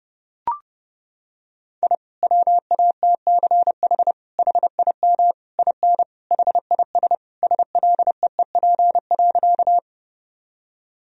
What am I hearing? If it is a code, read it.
Morse code, heard as a keyed tone, about 30 wpm, 700 Hz. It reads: I WATCH HIM IN HIS SLEEP.
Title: Sentences from Top 1000 Words - Review Part 1 - 30wpm